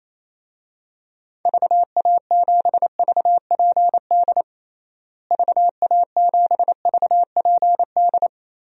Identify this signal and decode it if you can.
Morse code, heard as a keyed tone, about 28 wpm, 700 Hz. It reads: VA7VPD VA7VPD
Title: DXCC-100 Call Signs - 6 Characters - Repeated 3x - 28wpm